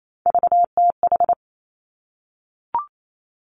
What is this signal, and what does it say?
Morse code, heard as a keyed tone, about 28 wpm, 700 Hz. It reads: VTH
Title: Three Challenging Letter-Number Combinations 28wpm